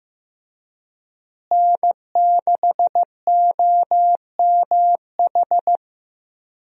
Morse code, heard as a keyed tone, about 15 wpm, 700 Hz. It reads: N6OMH
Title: USA Call Signs - 5 Characters - Repeated 3x - 15wpm